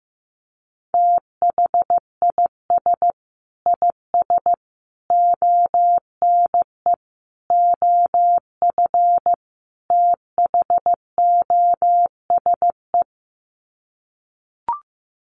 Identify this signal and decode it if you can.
Morse code, heard as a keyed tone, about 15 wpm, 700 Hz. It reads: THIS IS ONE OF THOSE
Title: Sets of 5 Words Encore2 15wpm